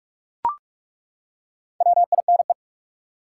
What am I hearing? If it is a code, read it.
Morse code, heard as a keyed tone, about 43 wpm, 700 Hz. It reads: WINE
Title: Four Letter Words - 45wpm